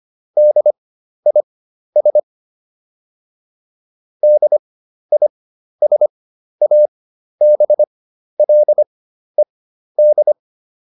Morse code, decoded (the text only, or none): DIS DISABLED